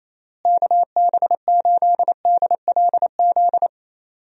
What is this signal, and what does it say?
Morse code, heard as a keyed tone, about 28 wpm, 700 Hz. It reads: KB8DLZ
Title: USA Call Signs - 6 Characters - Repeated 3x - 28wpm